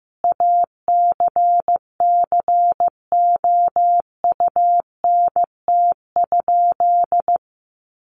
Morse code, heard as a keyed tone, about 15 wpm, 700 Hz. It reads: ACCOUNT?
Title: Sentences from Top 900 Words 15wpm